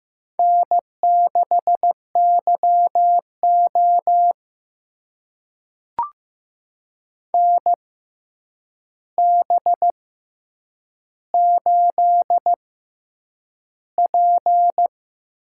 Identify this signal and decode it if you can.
Morse code, heard as a keyed tone, about 15 wpm, spaced as for 8 wpm, 700 Hz. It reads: N6YO NB8P
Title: USA Call Signs - 4 Characters - 6x Character Spacing - 15wpm